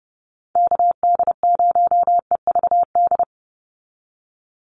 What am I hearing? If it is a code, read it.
Morse code, heard as a keyed tone, about 30 wpm, 700 Hz. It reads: KD0EVD
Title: USA Call Signs 30wpm